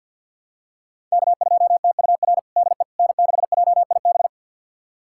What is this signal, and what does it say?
Morse code, heard as a keyed tone, about 51 wpm, 700 Hz. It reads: K2TUR DE N6JIB
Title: USA Callsign DE Callsign - 5 Characters - 50wpm